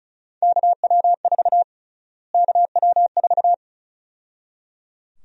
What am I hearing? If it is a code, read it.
Morse code, heard as a keyed tone, about 36 wpm, 700 Hz. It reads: KW4 KW4
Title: USA Call Sign Prefixes - Repeated 2x - 35wpm